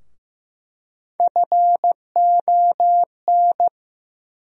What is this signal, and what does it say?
Morse code, heard as a keyed tone, about 15 wpm, 700 Hz. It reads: FON
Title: USA Call Sign Postfixes - 3 Letters 15wpm